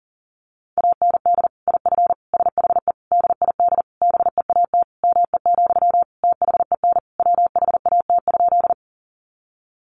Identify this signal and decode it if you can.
Morse code, heard as a keyed tone, about 40 wpm, 700 Hz. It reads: AND IF SHE DID BEAT ME, THEN WHAT?
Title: Sentences from Top 1300 Words 40wpm